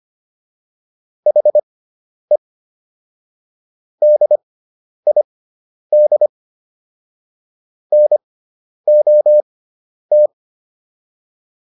Morse code, heard as a keyed tone, about 25 wpm, spaced as for 10 wpm, 600 Hz. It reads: HE DID NOT